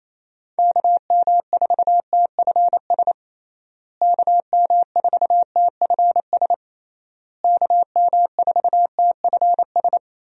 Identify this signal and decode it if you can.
Morse code, heard as a keyed tone, about 27 wpm, 700 Hz. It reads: KM4TFS KM4TFS KM4TFS
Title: USA Call Signs - 6 Characters - Repeated 3x - 28wpm